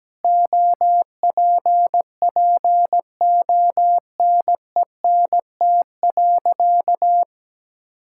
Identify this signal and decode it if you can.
Morse code, heard as a keyed tone, about 17 wpm, 700 Hz. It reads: OPPONENT.
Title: Pangram Challenge v2 - 17wpm